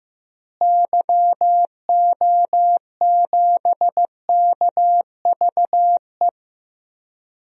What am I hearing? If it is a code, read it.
Morse code, heard as a keyed tone, about 15 wpm, 700 Hz. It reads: YO7KVE
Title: DXCC-100 Call Signs - 6 Characters - Repeated 3x - 15wpm